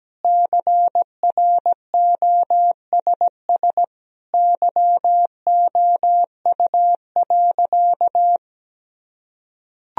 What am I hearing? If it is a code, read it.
Morse code, heard as a keyed tone, about 17 wpm, 700 Hz. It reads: CROSS YOU.
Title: Sentences from Top 1000 Words - Review Part 1 - 17wpm